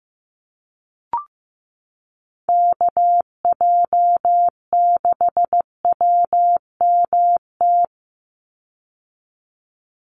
Morse code, heard as a keyed tone, about 15 wpm, 700 Hz. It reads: KJ6WMT
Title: USA Call Signs 15wpm